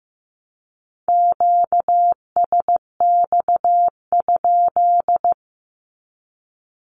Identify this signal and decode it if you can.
Morse code, heard as a keyed tone, about 15 wpm, 700 Hz. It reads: QSX?
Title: Q-Codes 15wpm